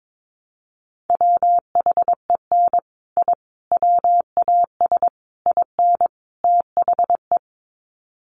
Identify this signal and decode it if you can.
Morse code, heard as a keyed tone, about 22 wpm, 700 Hz. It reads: WHEN I WAS IN THE
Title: Sets of 5 Words Encore2 22wpm